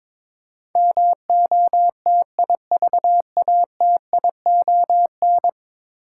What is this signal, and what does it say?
Morse code, heard as a keyed tone, about 22 wpm, 700 Hz. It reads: MOTIVATION
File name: practice / 22wpm / 10 Letter Words - First 3 Letters Given - 22wpm